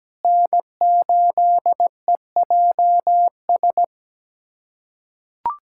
Morse code, heard as a keyed tone, about 17 wpm, 700 Hz. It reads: N8EJS
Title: USA Callsign DE Callsign 17wpm